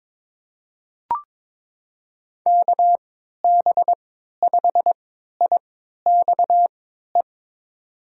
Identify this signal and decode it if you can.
Morse code, heard as a keyed tone, about 21 wpm, spaced as for 12 wpm, 700 Hz. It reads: KB5IXE
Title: USA Call Signs - 6 Characters - 3x Character Spacing - 22wpm